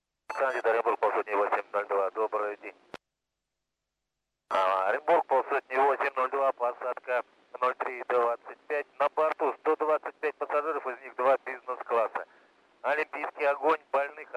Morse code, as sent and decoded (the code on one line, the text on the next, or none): none